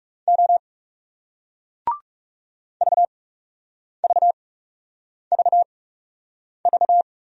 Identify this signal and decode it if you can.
Morse code, heard as a keyed tone, about 37 wpm, spaced as for 8 wpm, 700 Hz. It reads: O VVVV
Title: Speed-Racing Single Letter-Number 30wpm